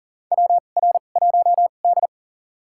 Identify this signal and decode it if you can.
Morse code, heard as a keyed tone, about 40 wpm, spaced as for 29 wpm, 700 Hz. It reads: WR1D
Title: USA Call Signs - 2x Character Spacing - 40wpm